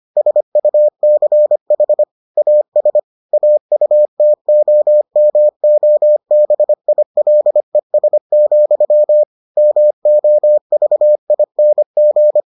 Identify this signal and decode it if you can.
Morse code, heard as a keyed tone, about 25 wpm, 600 Hz. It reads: SUCH AS AUTOMOBILES, MOVING